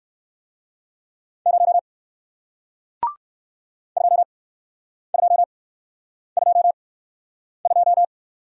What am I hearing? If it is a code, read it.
Morse code, heard as a keyed tone, about 56 wpm, 700 Hz. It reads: 0 2 2 2 2
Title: Speed-Racing Single Letter-Number 45wpm